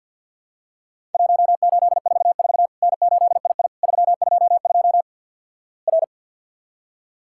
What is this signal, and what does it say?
Morse code, heard as a keyed tone, about 50 wpm, 700 Hz. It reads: 1844 N8II 322 R
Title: Worked All Europe DX Contest - QTC Exchange with Repeats - 50wpm